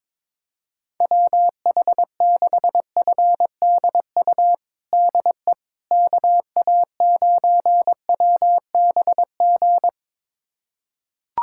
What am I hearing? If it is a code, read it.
Morse code, heard as a keyed tone, about 22 wpm, 700 Hz. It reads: WH6FDU DE KA9WBG
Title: USA Callsign DE Callsign - 6 Characters - 22wpm